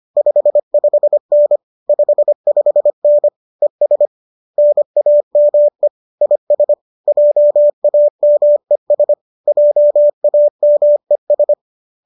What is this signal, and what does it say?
Morse code, heard as a keyed tone, about 25 wpm, 590 Hz. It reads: 55N 55N ES NAME IS JAMES JAMES